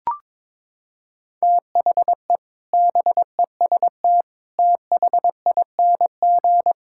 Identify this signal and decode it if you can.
Morse code, heard as a keyed tone, about 22 wpm, 700 Hz. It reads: THE BEST THING
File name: Sentences with 7-Word Phrases - 22wpm